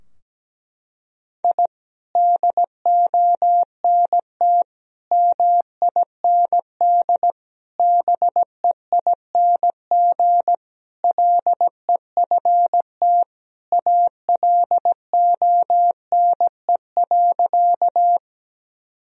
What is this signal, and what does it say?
Morse code, heard as a keyed tone, about 17 wpm, 700 Hz. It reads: I DONT MIND BEING LEFT ALONE.
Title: Sentences from Top 1000 Words - Review Part 3 - 17wpm